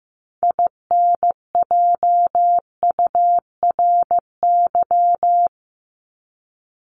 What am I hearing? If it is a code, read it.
Morse code, heard as a keyed tone, about 15 wpm, 700 Hz. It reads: INJURY
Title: Top 1751-2000 Words 15wpm